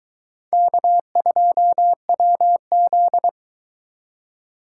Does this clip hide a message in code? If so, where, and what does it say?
Yes: Morse code from 0.5 to 3.3 s, heard as a keyed tone, about 22 wpm, 700 Hz. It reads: K2WZ